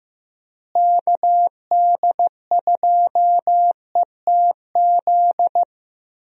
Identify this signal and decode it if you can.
Morse code, heard as a keyed tone, about 15 wpm, 700 Hz. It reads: KD2ETZ
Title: USA Call Signs - 6 Characters - Repeated 3x - 15wpm